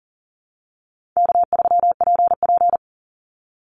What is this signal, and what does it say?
Morse code, heard as a keyed tone, about 40 wpm, 700 Hz. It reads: K3PP
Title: Easy Call Signs - 40wpm